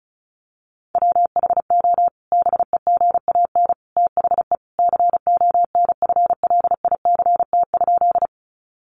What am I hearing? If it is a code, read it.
Morse code, heard as a keyed tone, about 34 wpm, 700 Hz. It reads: WHO BEGAN THE CONFLICT?
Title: Sentences from Top 1300 Words 35wpm